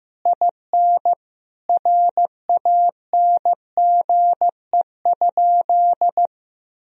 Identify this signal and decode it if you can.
Morse code, heard as a keyed tone, about 16 wpm, 700 Hz. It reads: IN RANGE?
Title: Sentences from Top 1000 Words - Review Part 4 - 15wpm